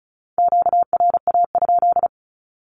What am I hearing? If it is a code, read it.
Morse code, heard as a keyed tone, about 36 wpm, 700 Hz. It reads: QRA?
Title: Q-Codes with Spoken Meaning - 35wpm